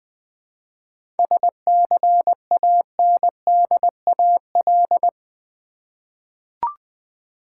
Morse code, heard as a keyed tone, about 20 wpm, 700 Hz. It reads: SCANDAL
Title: 7 Letter Words - First 3 Letters Given - 20wpm